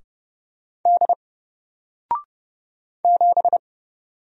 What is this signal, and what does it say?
Morse code, heard as a keyed tone, about 30 wpm, 700 Hz. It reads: D 7
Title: Single Letter-Number Alt 30wpm